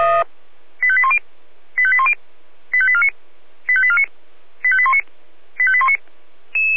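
Five-tone selective call (ZVEI), heard as quick runs of five tones, tones of about 70 ms. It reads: 85810 85810 85849 85849 85819 85819
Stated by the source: ZVEI five-tone selective call received on an SDR.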